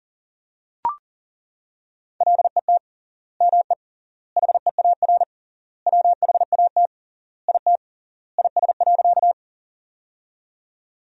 Morse code, heard as a keyed tone, about 40 wpm, 700 Hz. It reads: LET ME HEAR WHAT IT IS.
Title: 3x Word-Spacing - Sentences from Top 300 Words - 40wpm